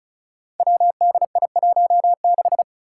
Morse code, heard as a keyed tone, about 35 wpm, 700 Hz. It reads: WDI16
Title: Five Challenging Letter-Number Combinations 35wpm